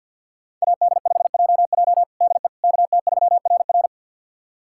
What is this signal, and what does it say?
Morse code, heard as a keyed tone, about 52 wpm, 700 Hz. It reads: AD5JJ DE KT3RR
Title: USA Callsign DE Callsign - 5 Characters - 50wpm